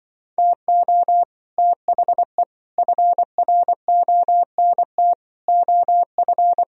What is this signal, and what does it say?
Morse code, heard as a keyed tone, about 24 wpm, 700 Hz. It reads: TO THE FRONT OF
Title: Speed-Racing - Sets of 5 Words 20wpm